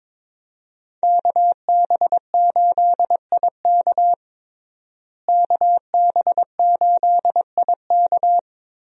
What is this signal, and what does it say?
Morse code, heard as a keyed tone, about 22 wpm, 700 Hz. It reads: KB8IK KB8IK
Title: USA Call Signs - 5 Characters - Repeated 3x - 22wpm